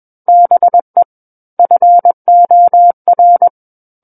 Morse code, heard as a keyed tone, about 21 wpm, 700 Hz. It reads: BE FOR
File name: Book One / 12-Chapter 12 - 21 WPM